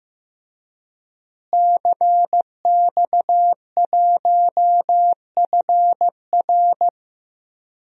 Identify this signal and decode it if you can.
Morse code, heard as a keyed tone, about 15 wpm, 700 Hz. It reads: CX1FR
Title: DXCC-100 Call Signs - 5 Characters - Repeated 2x - 15wpm